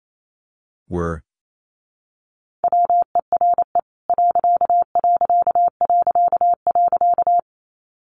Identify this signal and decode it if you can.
Morse code, heard as a keyed tone, about 28 wpm, 700 Hz. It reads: WERE ....
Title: A Sound Of Thunder 28wpm